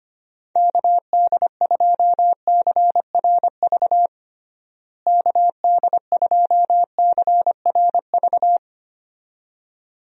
Morse code, heard as a keyed tone, about 25 wpm, 700 Hz. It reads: KD2CRV KD2CRV
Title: USA Call Signs - 6 Characters - Repeated 2x - 25wpm